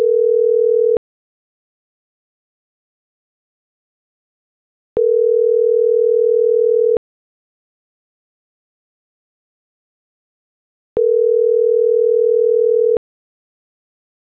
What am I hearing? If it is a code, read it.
Ringback tone (North America), 440 + 480 Hz.